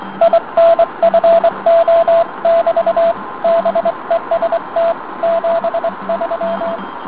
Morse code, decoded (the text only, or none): INFO<BT> BEST 73